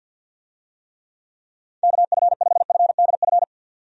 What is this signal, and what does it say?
Morse code, heard as a keyed tone, about 48 wpm, 700 Hz. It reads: KF5FDF
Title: USA Call Signs - 6 Characters - 50wpm